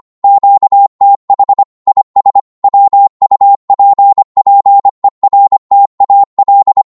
Morse code, heard as a keyed tone, about 25 wpm, 800 Hz. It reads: QTH IS WUPPERTAL
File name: Morse / qso-25wpm